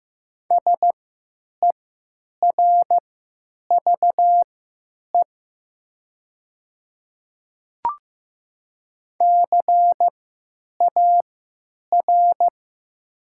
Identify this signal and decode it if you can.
Morse code, heard as a keyed tone, about 16 wpm, spaced as for 9 wpm, 700 Hz. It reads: SERVE CAR
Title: 3x Character Spacing - Top 500 Words - 15wpm